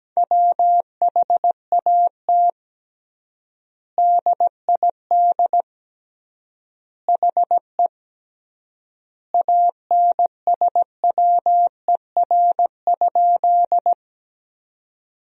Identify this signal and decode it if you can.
Morse code, heard as a keyed tone, about 17 wpm, 700 Hz. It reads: WHAT DID HE ANSWER?
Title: 3x Word-Spacing - Sentences from Top 300 Words - 17wpm